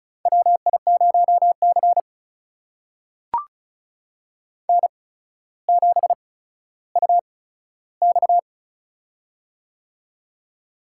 Morse code, heard as a keyed tone, about 34 wpm, spaced as for 15 wpm, 700 Hz. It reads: WI0C N7UX
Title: USA Call Signs - 4 Characters - 8x Character Spacing - 35wpm